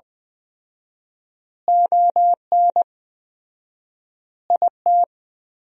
Morse code, heard as a keyed tone, about 21 wpm, 700 Hz. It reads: ON IT
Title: 4x Word-Spacing - Sentences from Top 400 Words - 20wpm